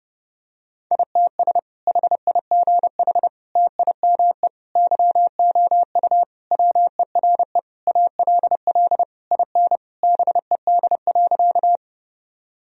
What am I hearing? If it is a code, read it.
Morse code, heard as a keyed tone, about 30 wpm, 700 Hz. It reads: ITS HIGH TIME YOU WERE ALL IN BED.
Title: Sentences from Top 1000 Words - Review Part 1 - 30wpm